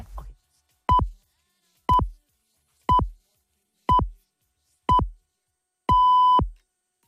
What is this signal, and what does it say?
Radio time signal (time pips). It is heard from 0.9 to 6.4 s.